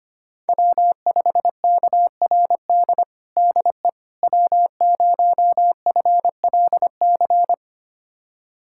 Morse code, heard as a keyed tone, about 25 wpm, 700 Hz. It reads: W5KRD DE W0FLC
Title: USA Callsign DE Callsign - 5 Characters - 25wpm